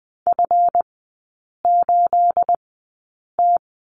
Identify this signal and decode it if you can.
Morse code, heard as a keyed tone, about 20 wpm, spaced as for 8 wpm, 690 Hz. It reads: F8T
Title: 212 Lesson 34e - Number 8 - Three Characters 20wpm